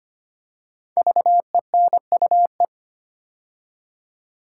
Morse code, heard as a keyed tone, about 25 wpm, 700 Hz. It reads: VENUE
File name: 5 Letter Words - First 3 Letters Given - 25wpm